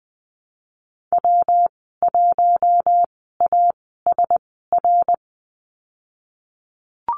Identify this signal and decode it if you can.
Morse code, heard as a keyed tone, about 20 wpm, spaced as for 15 wpm, 700 Hz. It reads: W1ASR
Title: 154 Lesson 25g - Letter p - Call Signs 20wpm